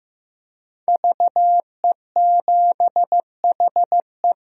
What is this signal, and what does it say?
Morse code, heard as a keyed tone, about 15 wpm, 700 Hz. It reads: VE7HE
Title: DXCC-100 Call Signs - 5 Characters - Repeated 2x - 15wpm